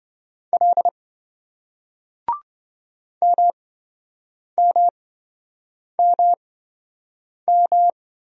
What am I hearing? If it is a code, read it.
Morse code, heard as a keyed tone, about 27 wpm, spaced as for 7 wpm, 700 Hz. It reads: L MMMM